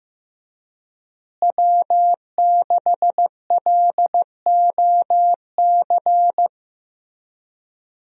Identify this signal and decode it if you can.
Morse code, heard as a keyed tone, about 15 wpm, 700 Hz. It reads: W6LOC
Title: USA Call Signs - 5 Characters - Repeated 2x - 15wpm